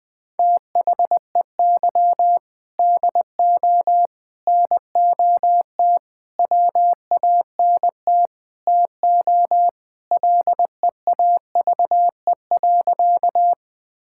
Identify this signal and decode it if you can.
Morse code, heard as a keyed tone, about 21 wpm, 700 Hz. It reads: THEY DO NOT WANT TO LEAVE.
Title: Sentences with 3-Word Phrases - Spoken Beginning - 20wpm